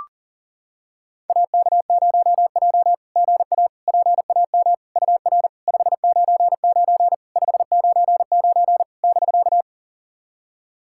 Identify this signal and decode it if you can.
Morse code, heard as a keyed tone, about 40 wpm, 690 Hz. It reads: AK0J GA PAM UR 599 599 <BK>
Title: SOTA QSO Practice 40wpm